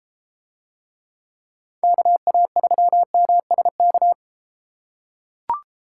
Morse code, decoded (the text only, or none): KA3MSK